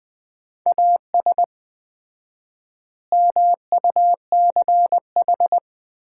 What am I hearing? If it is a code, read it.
Morse code, heard as a keyed tone, about 21 wpm, 700 Hz. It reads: AS MUCH
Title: 4x Word-Spacing - Sets of 3 Words - 20wpm